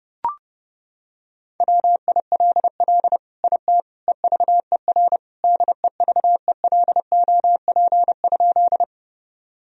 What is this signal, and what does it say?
Morse code, heard as a keyed tone, about 30 wpm, 700 Hz. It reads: WILL IT EVER DEVELOP?
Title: Sentences from Top 1000 Words - Review Part 2 - 30wpm